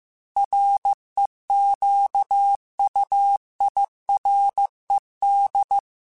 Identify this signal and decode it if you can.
Morse code, heard as a keyed tone, about 15 wpm, 800 Hz. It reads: REQUIRED
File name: Technical Subjects / Technical Subjects-mp3s / Sw Pwr Sup @13